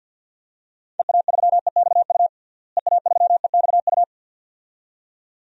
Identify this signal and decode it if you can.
Morse code, heard as a keyed tone, about 47 wpm, 700 Hz. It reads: EA3EXU EA3EXU
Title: DXCC-100 Call Signs - 6 Characters - Repeated 2x - 50wpm